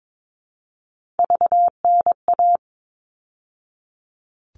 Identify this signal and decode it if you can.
Morse code, heard as a keyed tone, about 22 wpm, 700 Hz. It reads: VNA
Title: All Cut Numbers - 3-Digit 22wpm